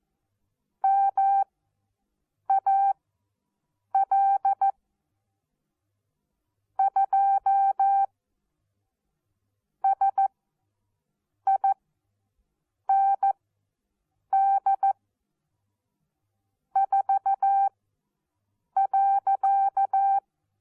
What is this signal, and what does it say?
Morse code, heard as a keyed tone, about 14 wpm, spaced as for 6 wpm, 780 Hz. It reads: MAL 2 SIND 4.
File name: Onlinelehrgang / morsen / DJ4UF Morsen14